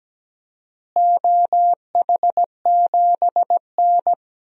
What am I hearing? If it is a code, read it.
Morse code, heard as a keyed tone, about 17 wpm, 700 Hz. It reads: OH7N
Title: DXCC-100 Call Signs - 4 Characters - Repeated 3x - 17wpm